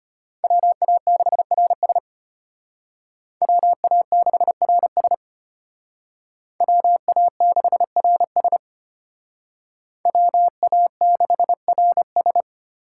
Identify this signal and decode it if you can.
Morse code, heard as a keyed tone, about 32 wpm, 700 Hz. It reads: WA6RS WA6RS WA6RS WA6RS